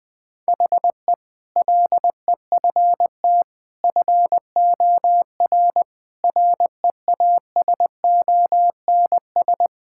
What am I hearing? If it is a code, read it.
Morse code, heard as a keyed tone, about 20 wpm, 700 Hz. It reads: HE LEFT FOR REASONS
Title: Sentences with 8-Word Phrases - Spoken Beginning - 20wpm